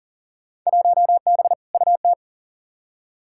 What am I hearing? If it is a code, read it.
Morse code, heard as a keyed tone, about 37 wpm, 700 Hz. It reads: 1B UT
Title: Field Day Exchanges - 40wpm